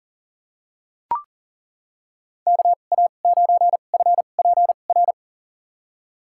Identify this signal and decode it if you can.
Morse code, heard as a keyed tone, about 40 wpm, spaced as for 29 wpm, 700 Hz. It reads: KA9FPR
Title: USA Call Signs - 6 Characters - 2x Character Spacing - 40wpm